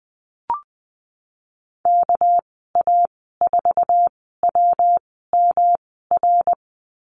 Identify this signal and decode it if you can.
Morse code, heard as a keyed tone, about 20 wpm, spaced as for 15 wpm, 700 Hz. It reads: KA4WMR
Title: 231 Lesson 37g - Letter x - Call Signs 20wpm